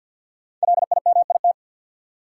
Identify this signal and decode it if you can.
Morse code, heard as a keyed tone, about 47 wpm, 700 Hz. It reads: LIMIT
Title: Sentences from Top 1500 Words - New Word Drill - 50wpm